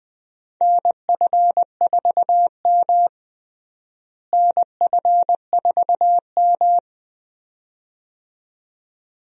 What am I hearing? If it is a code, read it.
Morse code, heard as a keyed tone, about 20 wpm, 700 Hz. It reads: NF4M NF4M
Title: USA Call Signs - 4 Characters - Repeated 2x - 20wpm